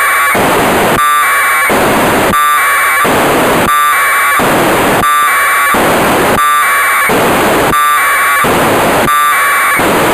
Packet radio (AX.25, as in APRS).
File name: Negai 1200bd packet received by JA6PL provided by JA0CAW